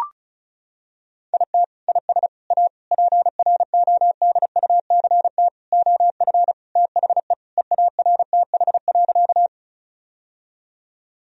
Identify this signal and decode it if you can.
Morse code, heard as a keyed tone, about 34 wpm, 700 Hz. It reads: IT IS A PRODUCT OF THE EARTH.
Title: Sentences from Top 1000 Words - Review Part 2 - 35wpm